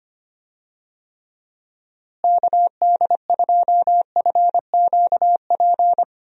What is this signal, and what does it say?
Morse code, heard as a keyed tone, about 25 wpm, 700 Hz. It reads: KD2FQP
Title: USA Call Signs - 6 Characters - Repeated 3x - 25wpm